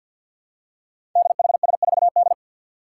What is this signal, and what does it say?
Morse code, heard as a keyed tone, about 52 wpm, 700 Hz. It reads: DHS4D
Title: Five Challenging Letter-Number Combinations 50wpm